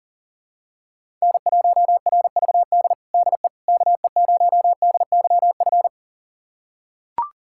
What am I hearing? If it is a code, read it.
Morse code, heard as a keyed tone, about 40 wpm, 700 Hz. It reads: N1RVD DE KE0DYF
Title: USA Callsign DE Callsign 40wpm